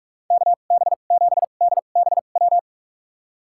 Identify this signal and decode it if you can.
Morse code, heard as a keyed tone, about 46 wpm, spaced as for 33 wpm, 700 Hz. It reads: KB7DBW
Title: USA Call Signs - 2x Character Spacing - 45wpm